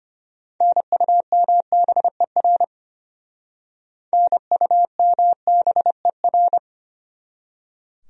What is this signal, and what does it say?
Morse code, heard as a keyed tone, about 32 wpm, 700 Hz. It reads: NUMBER NUMBER